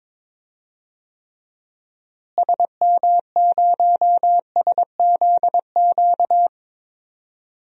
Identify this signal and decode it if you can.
Morse code, heard as a keyed tone, about 22 wpm, 700 Hz. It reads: SM0SZQ
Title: DXCC-100 Call Signs - 6 Characters - Repeated 3x - 22wpm